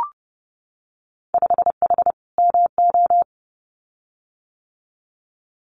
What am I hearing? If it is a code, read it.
Morse code, heard as a keyed tone, about 30 wpm, 700 Hz. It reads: 5H MO
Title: Winter Field Day Exchanges Encore 30wpm